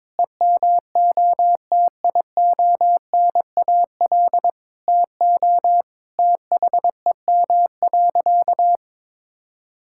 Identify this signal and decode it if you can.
Morse code, heard as a keyed tone, about 22 wpm, 700 Hz. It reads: EMOTIONAL TO THEM.
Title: Sentences from Top 1000 Words - Review Part 5 - 22wpm